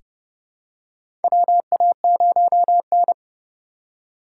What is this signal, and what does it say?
Morse code, heard as a keyed tone, about 32 wpm, 700 Hz. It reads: WA0N
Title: USA Call Signs - 4 Characters - 8x Character Spacing - 30wpm